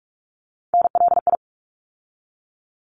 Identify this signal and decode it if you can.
Morse code, heard as a keyed tone, about 46 wpm, 700 Hz. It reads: NLI
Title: ARRL Sections 45wpm